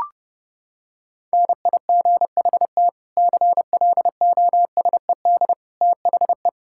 Morse code, heard as a keyed tone, about 32 wpm, 700 Hz. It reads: NIGHT CLOSED THE
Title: Sentences from Top 1600 Words 30wpm